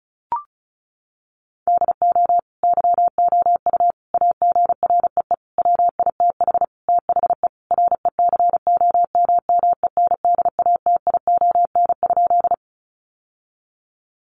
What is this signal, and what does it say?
Morse code, heard as a keyed tone, about 34 wpm, 700 Hz. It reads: DO YOU AGREE WITH THE RECOMMENDATION?
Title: Sentences from Top 1400 Words 35wpm